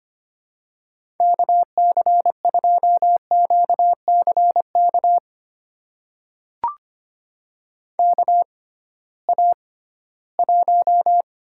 Morse code, heard as a keyed tone, about 25 wpm, spaced as for 16 wpm, 700 Hz. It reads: KC2QCK KA1
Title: USA Call Signs - 6 Characters - 6x Character Spacing - 25wpm